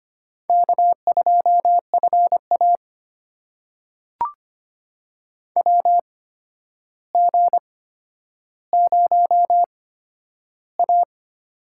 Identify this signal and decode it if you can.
Morse code, heard as a keyed tone, about 25 wpm, spaced as for 11 wpm, 700 Hz. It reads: K2FA WG0A
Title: USA Call Signs - 4 Characters - 8x Character Spacing - 25wpm